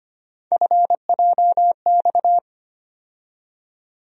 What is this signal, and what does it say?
Morse code, heard as a keyed tone, about 25 wpm, 700 Hz. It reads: FJX
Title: USA Call Sign Postfixes - 3 Letters 25wpm